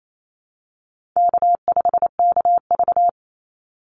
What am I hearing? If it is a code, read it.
Morse code, heard as a keyed tone, about 28 wpm, 700 Hz. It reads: K5KV
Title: Easy Call Signs Repeated 2x - 28wpm